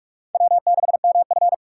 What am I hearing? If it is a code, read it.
Morse code, heard as a keyed tone, about 46 wpm, 700 Hz. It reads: W6MF
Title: USA Call Signs - 4 Characters - 6x Character Spacing - 45wpm